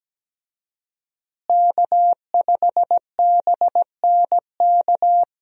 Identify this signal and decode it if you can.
Morse code, heard as a keyed tone, about 17 wpm, 700 Hz. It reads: K5BNK
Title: USA Call Signs - 5 Characters - Repeated 3x - 17wpm